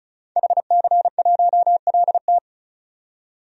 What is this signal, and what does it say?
Morse code, heard as a keyed tone, about 36 wpm, 700 Hz. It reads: HC1LT
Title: DXCC-100 Call Signs - 5 Characters - 35wpm